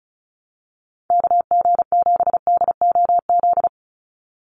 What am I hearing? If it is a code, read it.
Morse code, heard as a keyed tone, about 34 wpm, 700 Hz. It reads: KG7DOZ
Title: USA Call Signs 35wpm